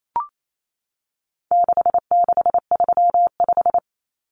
Morse code, heard as a keyed tone, about 28 wpm, 700 Hz. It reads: 6635
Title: Numbers - 4-Digit 28wpm